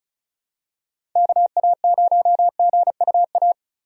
Morse code, heard as a keyed tone, about 34 wpm, 700 Hz. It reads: KA0GUA
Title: USA Call Signs - 6 Characters - 35wpm